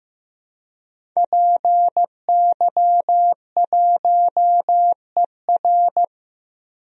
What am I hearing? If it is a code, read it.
Morse code, heard as a keyed tone, about 15 wpm, 700 Hz. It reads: PY1ER
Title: DXCC-100 Call Signs - 5 Characters - 15wpm